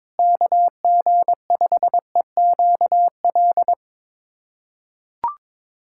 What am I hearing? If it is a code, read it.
Morse code, heard as a keyed tone, about 22 wpm, 700 Hz. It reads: KG5EQL
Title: USA Callsign DE Callsign - 6 Characters - 22wpm